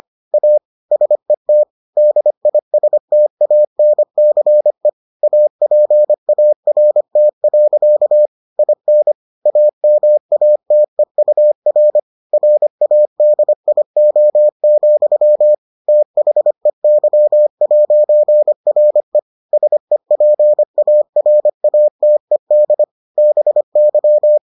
Morse code, heard as a keyed tone, about 25 wpm, 600 Hz. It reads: A SET DISTANCE APART. IN AMATEUR RADIO, THEY'RE SEPARATED BY